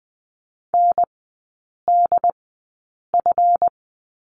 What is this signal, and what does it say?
Morse code, heard as a keyed tone, about 21 wpm, spaced as for 8 wpm, 690 Hz. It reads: NDF
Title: 145 Lesson 24e - Letter y - Three Characters 20wpm